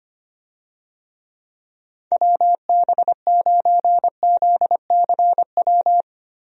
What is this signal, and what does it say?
Morse code, heard as a keyed tone, about 25 wpm, 700 Hz. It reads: WB9ZCW
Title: USA Call Signs - 6 Characters - Repeated 2x - 25wpm